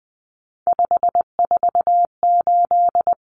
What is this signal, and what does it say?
Morse code, heard as a keyed tone, about 20 wpm, 700 Hz. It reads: 548
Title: Numbers - 3-Digit 20wpm